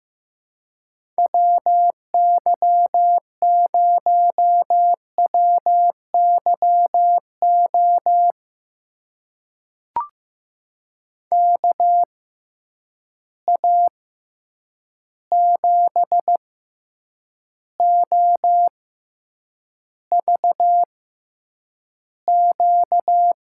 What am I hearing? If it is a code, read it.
Morse code, heard as a keyed tone, about 15 wpm, spaced as for 8 wpm, 700 Hz. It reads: WY0WYO KA7OVQ
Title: USA Call Signs - 6 Characters - 6x Character Spacing - 15wpm